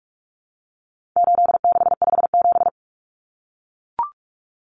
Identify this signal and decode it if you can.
Morse code, heard as a keyed tone, about 44 wpm, 700 Hz. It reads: 8657
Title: Common Cut Numbers - 4-Digit 45wpm